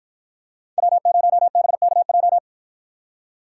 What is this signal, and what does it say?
Morse code, heard as a keyed tone, about 53 wpm, 700 Hz. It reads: W0BKJ